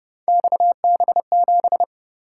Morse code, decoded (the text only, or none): XB7